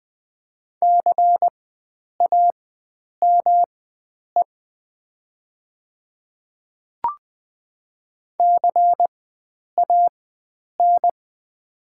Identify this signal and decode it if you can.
Morse code, heard as a keyed tone, about 20 wpm, spaced as for 9 wpm, 700 Hz. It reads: CAME CAN
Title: 4x Character Spacing - Top 500 Words - 20wpm